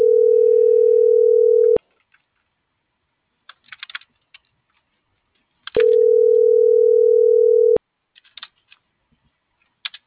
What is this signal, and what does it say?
Ringback tone (North America), 440 + 480 Hz.